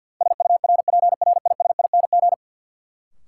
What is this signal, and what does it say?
Morse code, heard as a keyed tone, about 53 wpm, 700 Hz. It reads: SURPRISING
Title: Sets of 5 Words - New Word Drill - 50wpm